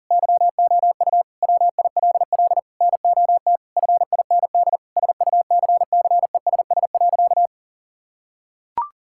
Morse code, heard as a keyed tone, about 40 wpm, 700 Hz. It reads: YOU WILL NOT FIND SUCCESS.
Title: Sentences from Top 1000 Words - Review Part 5 - 40wpm